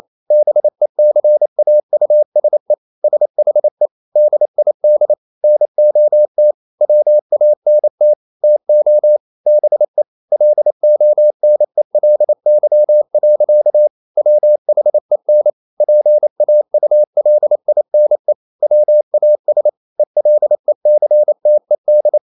Morse code, heard as a keyed tone, about 28 wpm, 600 Hz. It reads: BECAUSE SHE DID NOT WANT TO BE LONELY. WHEN PAULINE WAS ELECTED
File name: Never 0027wpm